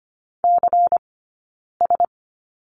Morse code, heard as a keyed tone, about 25 wpm, 700 Hz. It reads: C S
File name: course / 25:10wpm / 092 Lesson 16d - Letter c - Two Characters 25wpm